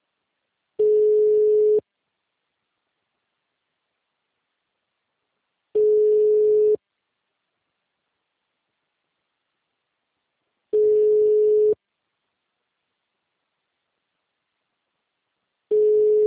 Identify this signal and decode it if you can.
Ringback tone (Europe, 425 Hz).
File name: out- -2005- - - .393